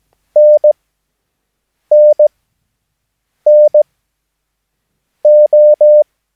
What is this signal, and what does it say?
Morse code, heard as a keyed tone, about 17 wpm, spaced as for 6 wpm, 600 Hz. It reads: NNNO
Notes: Övningsmaterial med alfabetet A till Ö samt siffror 0 till 9 enligt International/Continental Morse Code.